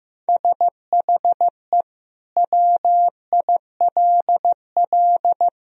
Morse code, heard as a keyed tone, about 15 wpm, 700 Hz. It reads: SHE WILL
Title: Sentences from Top 1000 Words - Review Part 2 - 15wpm